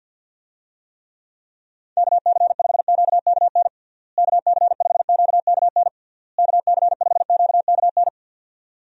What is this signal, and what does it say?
Morse code, heard as a keyed tone, about 48 wpm, 700 Hz. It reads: KC5QKN KC5QKN KC5QKN
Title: USA Call Signs - 6 Characters - Repeated 3x - 50wpm